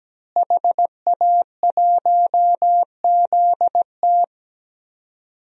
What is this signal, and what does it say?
Morse code, heard as a keyed tone, about 17 wpm, 700 Hz. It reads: HA1ZT
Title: DXCC-100 Call Signs - 5 Characters - Repeated 2x - 17wpm